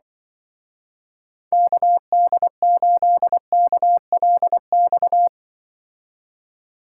Morse code, heard as a keyed tone, about 23 wpm, 700 Hz. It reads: KD8KLX
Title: Speed Racing - USA 2x3 Call Signs 20wpm